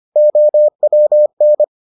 Morse code, heard as a keyed tone, about 25 wpm, 600 Hz. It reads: OWN